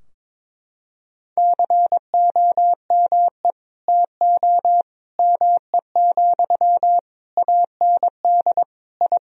Morse code, heard as a keyed tone, about 22 wpm, 700 Hz. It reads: COME TO ME, AND I
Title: Sentences from Top 1000 Words - Review Part 2 - 22wpm